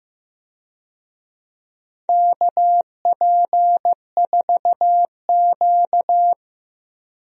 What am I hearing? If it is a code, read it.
Morse code, heard as a keyed tone, about 16 wpm, 700 Hz. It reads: KP4Q
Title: DXCC-100 Call Signs - 4 Characters - 15wpm